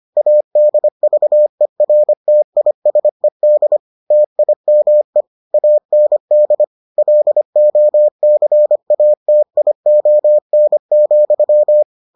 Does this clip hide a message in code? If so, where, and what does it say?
Yes: Morse code from 0.2 to 11.8 s, heard as a keyed tone, about 25 wpm, 600 Hz. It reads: ADVERTISED TIME AND LOCATION,